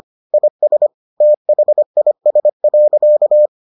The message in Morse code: IS THIS.

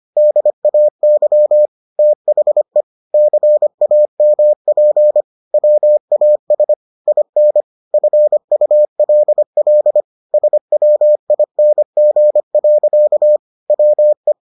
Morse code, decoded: DAY THE CAMP WAS IN FULL SWING. WE

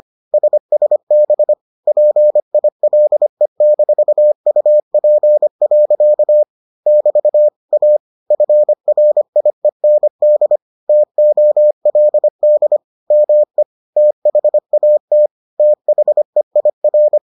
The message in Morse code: SSB PILE-UP. <BT> A FRIEND TOLD ME THAT THEIR

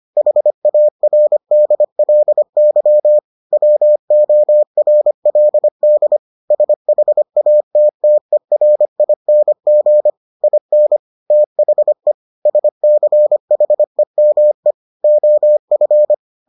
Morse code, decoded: HARDLY WORLD SHATTERING IN THE SCHEME OF

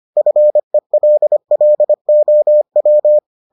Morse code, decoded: FELLOW

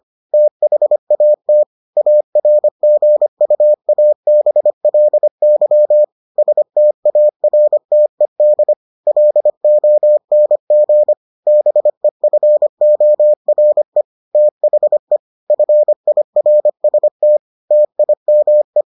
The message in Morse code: THAT ARGUABLY STARTED LONG BEFORE THE FIRST TIME

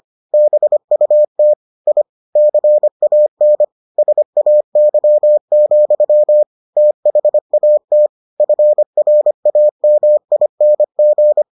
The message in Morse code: BUT I CAN SAY, THAT FRAMING

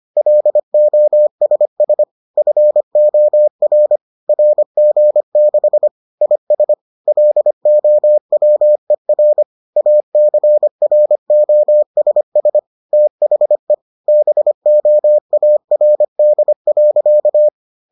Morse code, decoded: LOSS FOR RG6 IS LOWER ACROSS THE BOARD.

